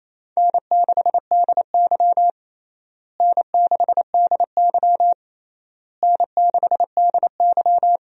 Morse code, heard as a keyed tone, about 28 wpm, 700 Hz. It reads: N6DY N6DY N6DY
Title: USA Call Signs - 4 Characters - Repeated 3x - 28wpm